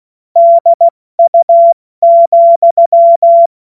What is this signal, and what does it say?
Morse code, heard as a keyed tone, about 16 wpm, 680 Hz. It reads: DU,
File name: Witz-28-Oktober-2025-70-F80